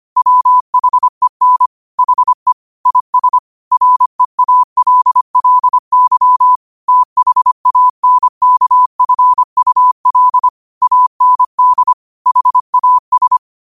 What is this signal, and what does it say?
Morse code, heard as a keyed tone, about 25 wpm, 1000 Hz. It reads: WHEN HE IS REALLY THANKFUL AND HAS
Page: Quotes for Fri, 15 Aug 2025 in Morse Code at 25 words per minute.